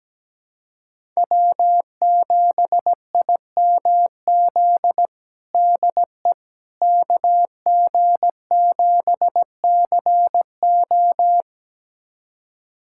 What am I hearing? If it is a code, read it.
Morse code, heard as a keyed tone, about 17 wpm, 700 Hz. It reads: W7IMZ DE KG7CO
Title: USA Callsign DE Callsign - 5 Characters - 17wpm